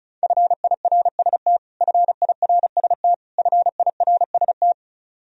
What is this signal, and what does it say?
Morse code, heard as a keyed tone, about 34 wpm, 700 Hz. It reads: FIRST FIRST FIRST
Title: Top 100 Words Repeated 35wpm